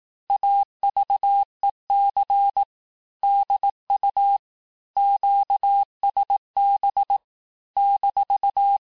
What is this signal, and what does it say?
Morse code, heard as a keyed tone, about 18 wpm, 800 Hz. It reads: AVEC DU QSB -
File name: cours-du-lundi files / cours-1594--FIN-de-QSO-vitesse-17mpm